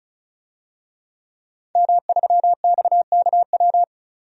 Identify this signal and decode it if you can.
Morse code, heard as a keyed tone, about 34 wpm, 700 Hz. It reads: M3XKW
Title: DXCC-100 Call Signs - 5 Characters - 35wpm